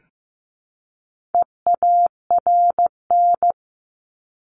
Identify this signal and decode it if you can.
Morse code, heard as a keyed tone, about 15 wpm, 700 Hz. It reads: EARN
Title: Top 1251-1500 Words 15wpm